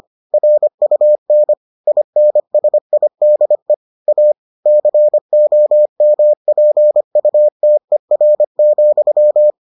Morse code, heard as a keyed tone, about 25 wpm, 600 Hz. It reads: RUN INSIDE A COMPUTER,